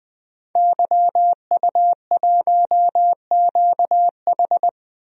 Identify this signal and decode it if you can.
Morse code, heard as a keyed tone, about 20 wpm, 700 Hz. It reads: YU1QH
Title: DXCC-100 Call Signs - 5 Characters - 20wpm